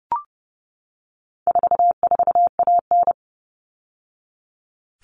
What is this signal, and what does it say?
Morse code, heard as a keyed tone, about 30 wpm, 700 Hz. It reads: 44AN
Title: All Cut Numbers - 4-Digit 30wpm